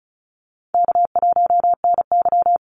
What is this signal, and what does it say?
Morse code, heard as a keyed tone, about 34 wpm, 700 Hz. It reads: K1NY
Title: Easy Call Signs Repeated 2x - 35wpm